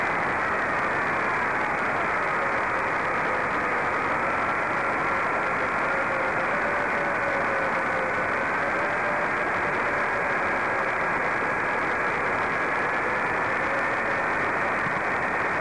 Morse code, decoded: none